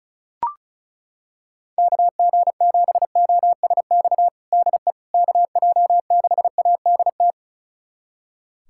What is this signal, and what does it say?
Morse code, heard as a keyed tone, about 36 wpm, 700 Hz. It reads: KG7OSX DE KJ6ADT
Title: USA Callsign DE Callsign 35wpm